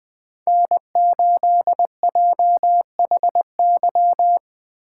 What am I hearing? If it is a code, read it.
Morse code, heard as a keyed tone, about 20 wpm, 700 Hz. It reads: N8JHY